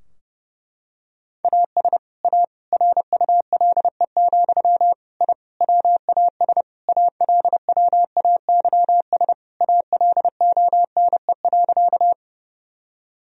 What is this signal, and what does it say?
Morse code, heard as a keyed tone, about 30 wpm, 700 Hz. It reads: AS A RULE, I WAS ALWAYS ALONE.
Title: Sentences from Top 1000 Words - Review Part 2 - 30wpm